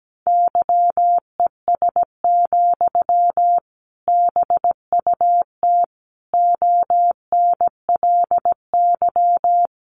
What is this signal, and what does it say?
Morse code, heard as a keyed tone, about 17 wpm, 700 Hz. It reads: YES, BUT ONLY
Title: Sentences from Top 500 Words 17wpm